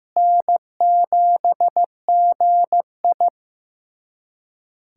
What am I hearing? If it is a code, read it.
Morse code, heard as a keyed tone, about 15 wpm, 700 Hz. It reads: N7GI
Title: USA Callsign DE Callsign - 4 Characters - 15wpm